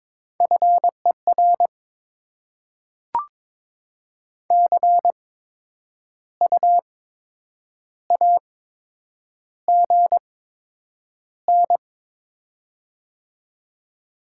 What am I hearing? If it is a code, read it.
Morse code, heard as a keyed tone, about 22 wpm, spaced as for 8 wpm, 700 Hz. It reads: FER CU AGN
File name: Top 20 QSO Elements - 8x Character Spacing - 22wpm